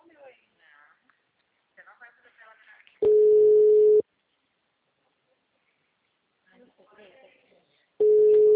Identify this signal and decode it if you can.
Ringback tone (Europe, 425 Hz), with speech.